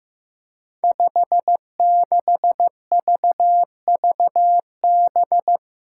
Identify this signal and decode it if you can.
Morse code, heard as a keyed tone, about 15 wpm, 700 Hz. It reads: 56VVB
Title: Five Challenging Letter-Number Combinations 15wpm